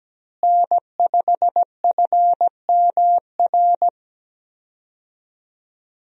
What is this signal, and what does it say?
Morse code, heard as a keyed tone, about 17 wpm, 700 Hz. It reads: N5FMR
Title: USA Call Signs - 5 Characters - Repeated 3x - 17wpm